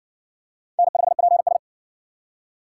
Morse code, heard as a keyed tone, about 55 wpm, 700 Hz. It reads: N5PS